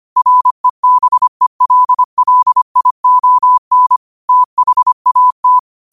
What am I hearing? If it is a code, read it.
Morse code, heard as a keyed tone, about 25 wpm, 1000 Hz. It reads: REBELLION THAT
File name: qotdfiles / 25wpm